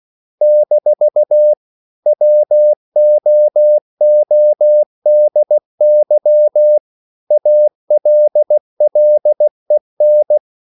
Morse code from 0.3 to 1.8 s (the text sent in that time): -